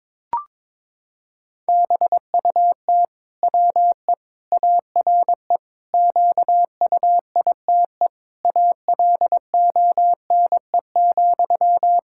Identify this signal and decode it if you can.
Morse code, heard as a keyed tone, about 22 wpm, 700 Hz. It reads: BUT WE ARE QUITE ALONE,
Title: Sentences from Top 1600 Words 22wpm